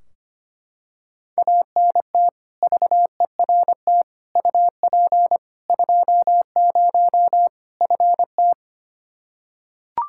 Morse code, heard as a keyed tone, about 25 wpm, 700 Hz. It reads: ANT VERT UP 20 FT
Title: Practice QSO - Exchange 2 - Info - 25wpm